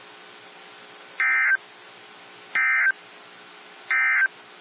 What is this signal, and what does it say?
Emergency Alert System (SAME) tones, heard as harsh buzzing bursts. It is heard from 1.1 to 4.4 s.
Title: NOAA Weather Radio - WXJ73